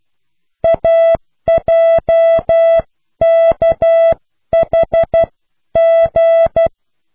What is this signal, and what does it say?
Morse code, heard as a keyed tone, about 12 wpm, spaced as for 10 wpm, 660 Hz. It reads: AJKHG